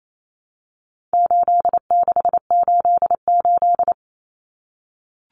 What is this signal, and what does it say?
Morse code, heard as a keyed tone, about 28 wpm, 700 Hz. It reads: 8688
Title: Numbers - 4-Digit 28wpm